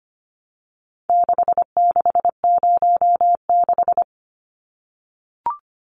Morse code, heard as a keyed tone, about 25 wpm, 700 Hz. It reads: 6606
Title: Numbers - 4-Digit 25wpm